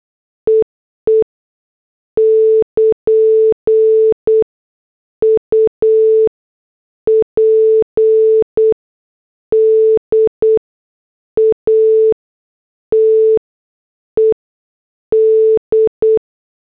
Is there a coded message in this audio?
Yes: Morse code from 0.5 to 16.2 s, heard as a keyed tone, about 8 wpm, 440 Hz. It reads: EE <KN> U P D A T E D